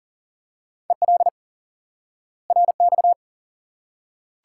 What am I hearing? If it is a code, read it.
Morse code, heard as a keyed tone, about 40 wpm, 700 Hz. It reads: EL RX
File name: IQE Course 3 - Lesson 89 - YAGI - Two Elements - 40wpm